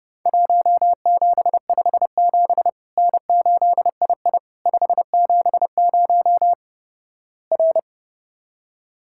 Morse code, heard as a keyed tone, about 30 wpm, 700 Hz. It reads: 1757 N8II 570 R
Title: Worked All Europe DX Contest - QTC Exchange - 30wpm